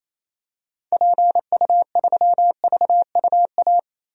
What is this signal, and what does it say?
Morse code, heard as a keyed tone, about 27 wpm, 700 Hz. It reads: PU3VUA
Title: DXCC-100 Call Signs - 6 Characters - 28wpm